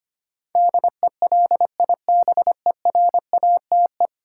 Morse code, heard as a keyed tone, about 25 wpm, 700 Hz. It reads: DELIBERATE
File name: Ten Letter Words - 25wpm